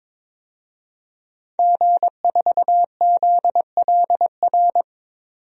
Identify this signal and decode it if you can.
Morse code, heard as a keyed tone, about 22 wpm, 700 Hz. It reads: G4ZLR
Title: DXCC-100 Call Signs - 5 Characters - 22wpm